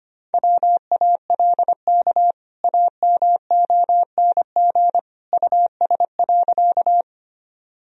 Morse code, heard as a keyed tone, about 25 wpm, 700 Hz. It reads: WALK AMONG US.
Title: Sentences from Top 1000 Words - Review Part 3 - 25wpm